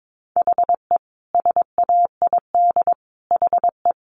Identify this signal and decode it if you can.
Morse code, heard as a keyed tone, about 22 wpm, 700 Hz. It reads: HE SAID HE
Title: Sentences from Top 1200 Words 22wpm